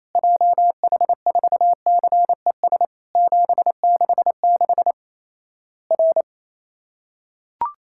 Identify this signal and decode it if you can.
Morse code, heard as a keyed tone, about 28 wpm, 700 Hz. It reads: JH4CES 766 R
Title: Worked All Europe DX Contest - QTC Exchange - 28wpm